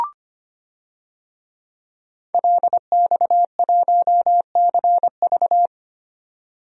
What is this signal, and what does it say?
Morse code, heard as a keyed tone, about 25 wpm, 700 Hz. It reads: LX1CV
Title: DXCC-100 Call Signs - 5 Characters - 25wpm